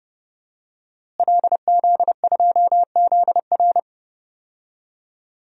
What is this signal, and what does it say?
Morse code, heard as a keyed tone, about 29 wpm, 700 Hz. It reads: LZ2ZR
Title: DXCC-100 Call Signs - 5 Characters - Repeated 3x - 30wpm